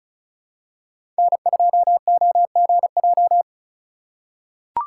Morse code, heard as a keyed tone, about 36 wpm, 700 Hz. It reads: N2OGJ
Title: USA Call Signs - 5 Characters - 4x Character Spacing - 35wpm